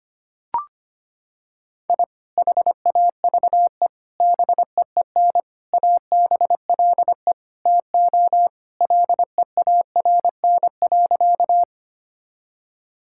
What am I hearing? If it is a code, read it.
Morse code, heard as a keyed tone, about 25 wpm, 700 Hz. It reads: I HAVE BEEN ABLE TO LEARN.
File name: practice / 25wpm / Sentences with 3-Word Phrases - 25wpm